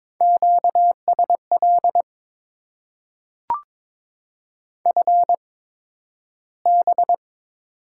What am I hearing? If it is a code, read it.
Morse code, heard as a keyed tone, about 22 wpm, spaced as for 12 wpm, 700 Hz. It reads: QSL FB
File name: Top 90 QSO Elements - 8x Character Spacing - 22wpm